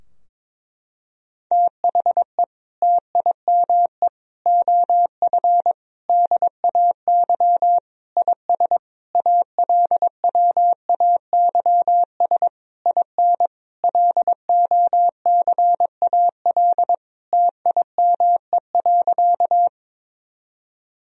Morse code, heard as a keyed tone, about 22 wpm, 700 Hz. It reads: THE TIME OF DAY IS ALWAYS IN LOCAL TIME.
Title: Sentences from Top 1000 Words - Review Part 3 - 22wpm